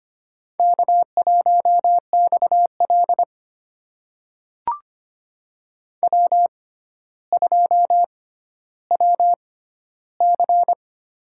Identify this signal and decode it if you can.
Morse code, heard as a keyed tone, about 25 wpm, spaced as for 13 wpm, 700 Hz. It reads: K1XL W2WC
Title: USA Call Signs - 4 Characters - 6x Character Spacing - 25wpm